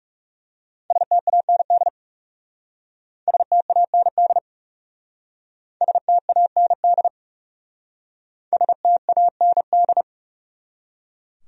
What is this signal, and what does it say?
Morse code, heard as a keyed tone, about 40 wpm, 700 Hz. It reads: STAND STAND STAND STAND